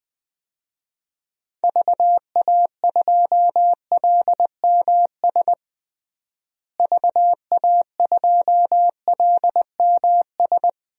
Morse code, heard as a keyed tone, about 20 wpm, 700 Hz. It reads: VA2LMS VA2LMS
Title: DXCC-100 Call Signs - 6 Characters - Repeated 2x - 20wpm